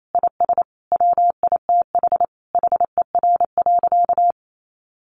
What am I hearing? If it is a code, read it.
Morse code, heard as a keyed tone, about 28 wpm, 700 Hz. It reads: IS WITH HER.
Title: Sentences from Top 100 Words 28wpm